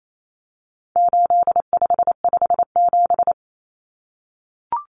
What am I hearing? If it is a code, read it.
Morse code, heard as a keyed tone, about 28 wpm, 700 Hz. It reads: 8557
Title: Numbers - 4-Digit 28wpm